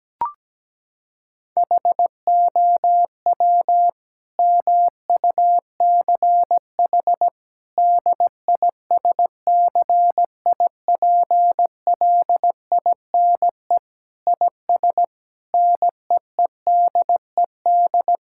Morse code, heard as a keyed tone, about 17 wpm, 700 Hz. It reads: HOW MUCH DISCIPLINE IS NEEDED
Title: Sentences from Top 1000 Words - Review Part 6 - 17wpm